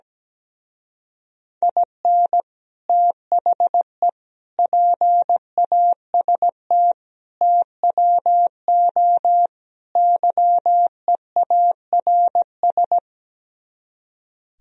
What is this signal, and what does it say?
Morse code, heard as a keyed tone, about 17 wpm, 700 Hz. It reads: IN THE PAST TWO YEARS
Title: Speed-Racing - Sets of 5 Words 17wpm